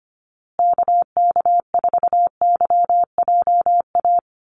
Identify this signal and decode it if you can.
Morse code, heard as a keyed tone, about 25 wpm, 700 Hz. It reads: KK4YJA
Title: USA Call Signs Repeated 2x 25wpm